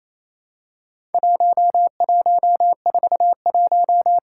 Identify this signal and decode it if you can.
Morse code, heard as a keyed tone, about 28 wpm, 700 Hz. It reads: 1141
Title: Worked All Europe DX Contest - QTC Exchange - 28wpm